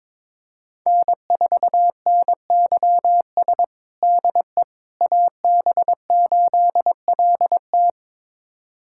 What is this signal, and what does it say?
Morse code, heard as a keyed tone, about 22 wpm, 700 Hz. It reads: N4NYS DE AB8LT
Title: USA Callsign DE Callsign - 5 Characters - 22wpm